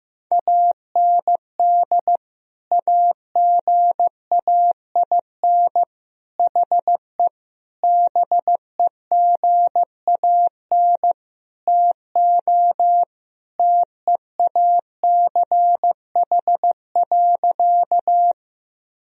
Morse code, heard as a keyed tone, about 15 wpm, 700 Hz. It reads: AND AGAIN HE BEGAN TO TEACH.
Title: Sentences from Top 1000 Words - Review Part 2 - 15wpm